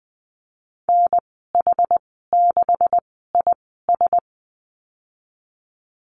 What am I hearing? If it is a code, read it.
Morse code, heard as a keyed tone, about 20 wpm, spaced as for 14 wpm, 700 Hz. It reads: NH6IS
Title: 154 Lesson 25g - Letter p - Call Signs 20wpm